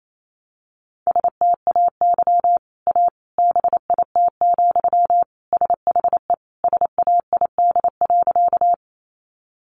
Morse code, heard as a keyed tone, about 28 wpm, 700 Hz. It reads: STAY A BIT, SHE SAID.
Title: Sentences from Top 900 Words 28wpm